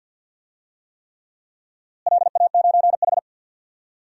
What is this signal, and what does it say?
Morse code, heard as a keyed tone, about 48 wpm, 700 Hz. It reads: LA9H
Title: DXCC-100 Call Signs - 4 Characters - Repeated 3x - 50wpm